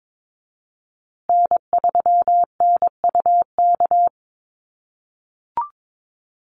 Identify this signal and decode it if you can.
Morse code, heard as a keyed tone, about 22 wpm, 700 Hz. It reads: N3NUK
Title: USA Call Signs Repeated 2x 22wpm